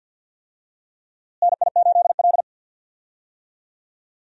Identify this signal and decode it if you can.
Morse code, heard as a keyed tone, about 52 wpm, 700 Hz. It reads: NI8L
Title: USA Call Signs - 4 Characters - Repeated 3x - 50wpm